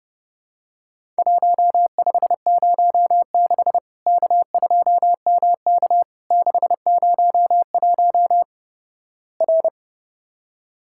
Morse code, heard as a keyed tone, about 29 wpm, 700 Hz. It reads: 1506 K2MK 601 R
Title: Worked All Europe DX Contest - QTC Exchange - 30wpm